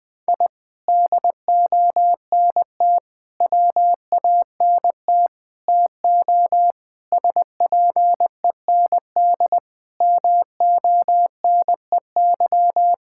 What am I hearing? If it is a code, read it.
Morse code, heard as a keyed tone, about 20 wpm, 700 Hz. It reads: I DONT WANT TO SPEND MONEY
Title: Sentences from Top 1000 Words - Review Part 5 - 20wpm